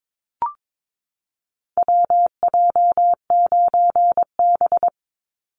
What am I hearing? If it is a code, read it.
Morse code, heard as a keyed tone, about 22 wpm, 700 Hz. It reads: WJ9B
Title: Call Signs - 22wpm